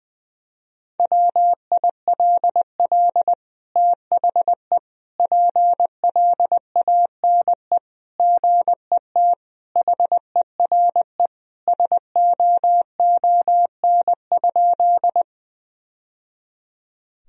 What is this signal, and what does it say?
Morse code, heard as a keyed tone, about 20 wpm, 700 Hz. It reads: WILL THE PLANE GET HERE SOON?
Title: Sentences from Top 1000 Words - Review Part 3 - 20wpm